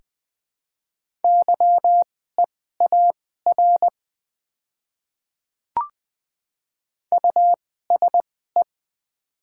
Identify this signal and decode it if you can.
Morse code, heard as a keyed tone, about 20 wpm, spaced as for 14 wpm, 700 Hz. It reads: YEAR USE
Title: 2x Character Spacing - Top 500 Words - 20wpm